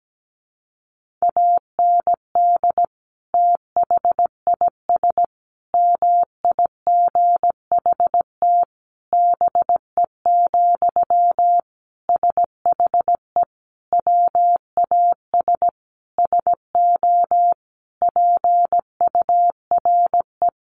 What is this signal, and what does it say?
Morse code, heard as a keyed tone, about 17 wpm, 700 Hz. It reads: AND THIS MIGHT BE, SHE WAS SO PURE
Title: Sentences from Top 1500 Words 17wpm